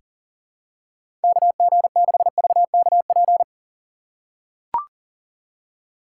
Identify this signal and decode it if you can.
Morse code, heard as a keyed tone, about 40 wpm, 700 Hz. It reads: KG6VKP
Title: USA Call Signs - 6 Characters - 6x Character Spacing - 40wpm